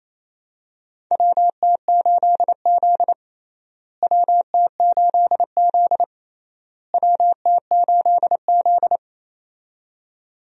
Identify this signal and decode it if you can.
Morse code, heard as a keyed tone, about 28 wpm, 700 Hz. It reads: WT8Z WT8Z WT8Z
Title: USA Call Signs - 4 Characters - Repeated 3x - 28wpm